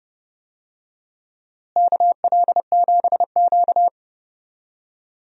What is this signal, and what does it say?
Morse code, heard as a keyed tone, about 32 wpm, 700 Hz. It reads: KL7Q
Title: DXCC-100 Call Signs - 4 Characters - Repeated 3x - 30wpm